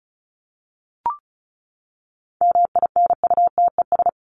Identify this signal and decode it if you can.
Morse code, heard as a keyed tone, about 34 wpm, 700 Hz. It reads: MINUTES
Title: ES:S Suffix - Top 100 Words with Base Words 35wpm